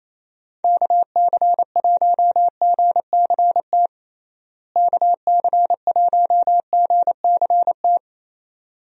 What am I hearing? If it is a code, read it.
Morse code, heard as a keyed tone, about 28 wpm, 700 Hz. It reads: KC1GCT KC1GCT
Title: USA Call Signs - 6 Characters - Repeated 3x - 28wpm